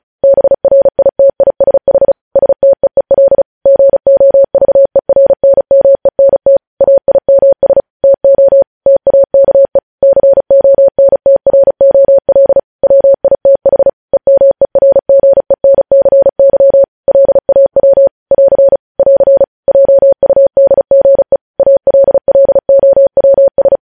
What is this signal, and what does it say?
Morse code, heard as a keyed tone, about 36 wpm, 550 Hz. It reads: BRITISH STEEL GOVERNMENT AIMS TO TAKE CONTROL WITH EMERGENCY LAW <AR> <AR> JUDGE ALLOWS
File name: podcast / News headlines-1x-35x35